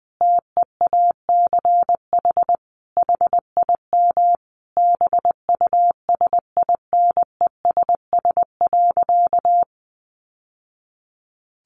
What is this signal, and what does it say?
Morse code, heard as a keyed tone, about 20 wpm, 700 Hz. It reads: TEACH HIM BUSINESS.
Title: Sentences from Top 600 Words 20wpm